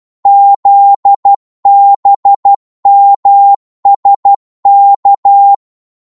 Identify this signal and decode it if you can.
Morse code, heard as a keyed tone, about 12 wpm, 800 Hz. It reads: ZBMSK
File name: Morse / ltr-12wpm